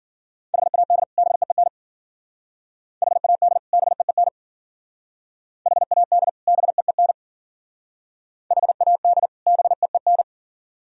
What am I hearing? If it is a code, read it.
Morse code, heard as a keyed tone, about 49 wpm, 700 Hz. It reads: HAD BEEN HAD BEEN HAD BEEN HAD BEEN
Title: Speed-Racing - Sets of 2 Words 40wpm